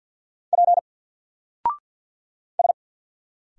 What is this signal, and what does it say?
Morse code, heard as a keyed tone, about 50 wpm, 700 Hz. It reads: P S
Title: Single Character (Letter:Number:Punctuation:Prosign) 50wpm